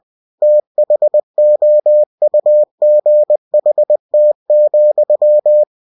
Morse code, heard as a keyed tone, about 20 wpm, 600 Hz. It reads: THOUGHT,
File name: cw-books / winnie-the-pooh / winnie-20wpm-page-0026